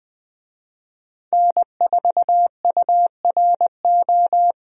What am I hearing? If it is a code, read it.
Morse code, heard as a keyed tone, about 20 wpm, 700 Hz. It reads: N4URO
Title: USA Call Signs - 5 Characters - 20wpm